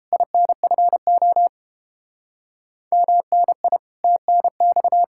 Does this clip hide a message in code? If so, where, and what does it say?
Yes: Morse code from 0.1 to 5.0 s, heard as a keyed tone, about 32 wpm, 700 Hz. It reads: INFO MNI TNX